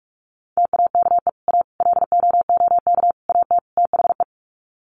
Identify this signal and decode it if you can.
Morse code, heard as a keyed tone, about 44 wpm, 700 Hz. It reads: TAKE A LOOK AT THE
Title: Sets of 5 Words 45wpm